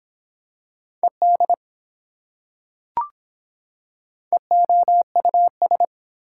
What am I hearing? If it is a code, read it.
Morse code, heard as a keyed tone, about 24 wpm, 700 Hz. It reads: ED EOUS